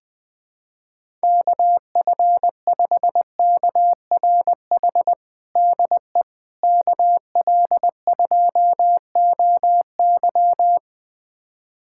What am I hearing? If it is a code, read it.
Morse code, heard as a keyed tone, about 20 wpm, 700 Hz. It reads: KF5KRH DE KL2OY
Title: USA Callsign DE Callsign 20wpm